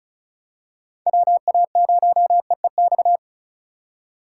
Morse code, heard as a keyed tone, about 36 wpm, 700 Hz. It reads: WA0EEX
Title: USA Call Signs - 6 Characters - 3x Character Spacing - 35wpm